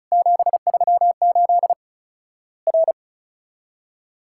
Morse code, heard as a keyed tone, about 36 wpm, 700 Hz. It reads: 738 R
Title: Worked All Europe DX Contest - QTC Exchange - 35wpm